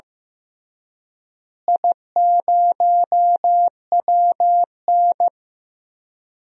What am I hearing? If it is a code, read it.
Morse code, heard as a keyed tone, about 16 wpm, 700 Hz. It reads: I0WN
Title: DXCC-100 Call Signs - 4 Characters - Repeated 3x - 15wpm